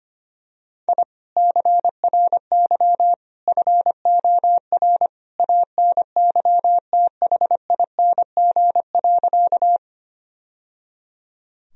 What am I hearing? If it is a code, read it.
Morse code, heard as a keyed tone, about 25 wpm, 700 Hz. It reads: I CRY FOR ANYTHING.
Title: Sentences from Top 1000 Words - Review Part 2 - 25wpm